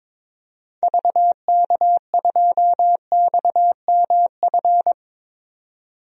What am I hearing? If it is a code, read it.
Morse code, heard as a keyed tone, about 22 wpm, 700 Hz. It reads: VK2XMF
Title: DXCC-100 Call Signs - 6 Characters - 22wpm